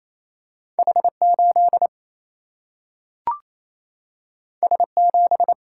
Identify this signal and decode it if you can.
Morse code, heard as a keyed tone, about 28 wpm, 700 Hz. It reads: H8 S7
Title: Single-Letter Single-Number 28wpm